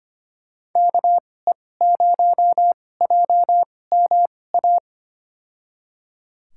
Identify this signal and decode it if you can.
Morse code, heard as a keyed tone, about 25 wpm, spaced as for 18 wpm, 700 Hz. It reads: KE0JMA
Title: USA Call Signs - 6 Characters - 2x Character Spacing - 25wpm